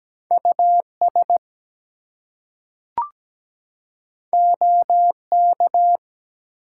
Morse code, heard as a keyed tone, about 17 wpm, 700 Hz. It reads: US OK
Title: Two Letter Words - 17wpm